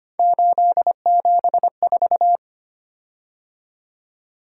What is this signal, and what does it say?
Morse code, heard as a keyed tone, about 25 wpm, 700 Hz. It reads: 874
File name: Three Letter-Number Combinations - 25wpm